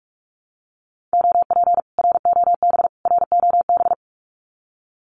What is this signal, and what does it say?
Morse code, heard as a keyed tone, about 46 wpm, 700 Hz. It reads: OP ROB ROB
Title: Common QSO Phrases Encore 45wpm